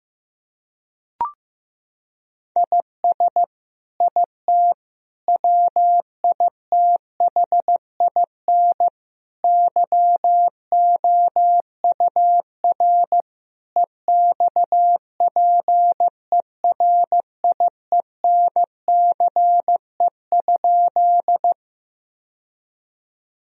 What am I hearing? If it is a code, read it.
Morse code, heard as a keyed tone, about 16 wpm, 700 Hz. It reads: IS IT WITHIN YOUR EXPERIENCE?
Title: Sentences from Top 1000 Words - Review Part 4 - 15wpm